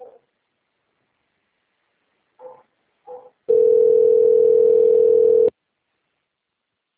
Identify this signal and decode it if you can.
Ringback tone (North America), 440 + 480 Hz.